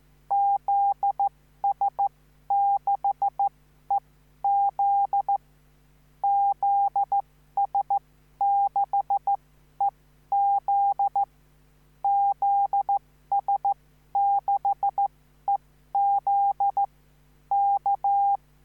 Morse code, heard as a keyed tone, about 14 wpm, spaced as for 11 wpm, 800 Hz. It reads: ZS6EZ ZS6EZ ZS6EZ K
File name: CQvonZS6EZ